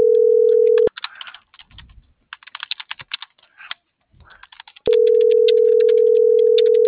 Ringback tone (North America), 440 + 480 Hz.